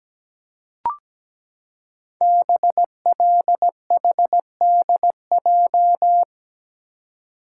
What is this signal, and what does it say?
Morse code, heard as a keyed tone, about 17 wpm, 700 Hz. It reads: BLHDJ
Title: Code Groups - 5 Letters - 17wpm